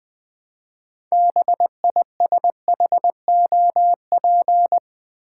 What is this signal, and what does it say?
Morse code, heard as a keyed tone, about 20 wpm, 700 Hz. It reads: BISHOP
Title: 6 Letter Words - First 3 Letters Given - 20wpm